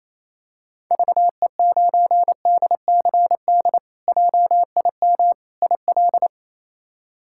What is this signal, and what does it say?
Morse code, heard as a keyed tone, about 28 wpm, 700 Hz. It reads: VE9DCD JIM IL
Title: K1USN SST QSO Practice - 28wpm